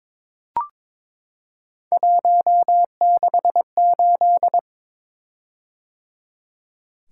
Morse code, heard as a keyed tone, about 22 wpm, 700 Hz. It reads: 168
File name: practice / 22wpm / Three Letter-Number Combinations - 22wpm